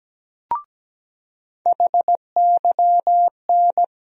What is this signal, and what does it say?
Morse code, heard as a keyed tone, about 17 wpm, 700 Hz. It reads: HYN
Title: USA Call Sign Postfixes - 3 Letters 17wpm